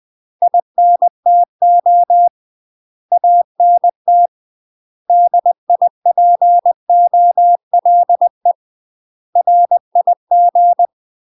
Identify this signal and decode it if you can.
Morse code, heard as a keyed tone, about 20 wpm, 700 Hz. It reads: INTO ANT DIPOLE RIG
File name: UK MASTER QSOs TRAINING / 20WPM / UK MASTER QSOs 0002